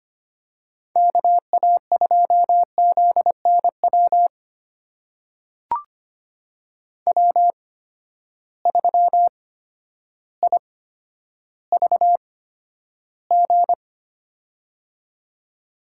Morse code, heard as a keyed tone, about 25 wpm, spaced as for 11 wpm, 700 Hz. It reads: KA2ZNW W3IVG
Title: USA Call Signs - 8x Character Spacing - 25wpm